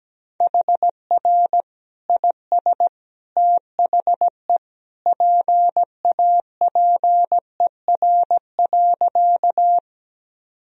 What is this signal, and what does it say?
Morse code, heard as a keyed tone, about 17 wpm, 700 Hz. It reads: HR IS THE PAPER.
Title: Sentences from Top 300 Words and Abbreviations 17wpm